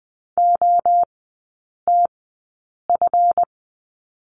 Morse code, heard as a keyed tone, about 20 wpm, spaced as for 8 wpm, 700 Hz. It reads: OTF
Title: 138 Lesson 23e - Letter f - Three Characters 20wpm